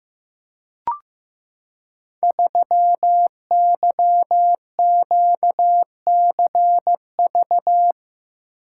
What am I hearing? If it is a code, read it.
Morse code, heard as a keyed tone, about 15 wpm, 700 Hz. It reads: 3YQCV
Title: Code Groups - 5 Letter:Numbers - 15wpm